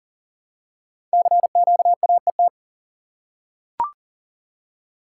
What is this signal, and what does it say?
Morse code, heard as a keyed tone, about 38 wpm, 700 Hz. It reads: CQAET
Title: Code Groups - 5 Letter:Numbers - 40wpm